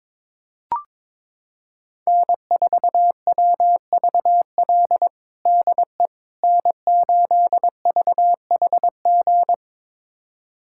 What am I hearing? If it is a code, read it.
Morse code, heard as a keyed tone, about 22 wpm, 700 Hz. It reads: N4WVL DE N8VHG
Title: USA Callsign DE Callsign - 5 Characters - 22wpm